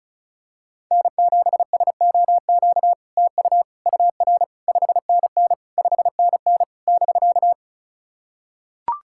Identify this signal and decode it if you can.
Morse code, heard as a keyed tone, about 37 wpm, 700 Hz. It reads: N7SOQ TU UR 5NN 5NN <BK>
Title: SOTA QSO Practice 35wpm